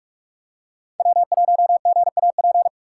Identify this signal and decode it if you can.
Morse code, heard as a keyed tone, about 46 wpm, 700 Hz. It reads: W1GAP
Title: USA Call Signs - 5 Characters - 2x Character Spacing - 45wpm